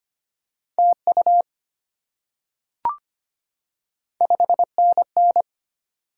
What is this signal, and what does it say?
Morse code, heard as a keyed tone, about 25 wpm, 700 Hz. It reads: TU 5NN
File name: Top 20 QSO Elements - 25wpm